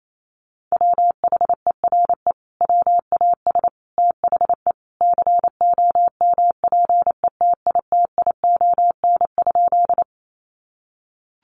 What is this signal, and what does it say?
Morse code, heard as a keyed tone, about 28 wpm, 700 Hz. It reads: WHERE WAS THE COMPETITION?
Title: Sentences from Top 1000 Words 28wpm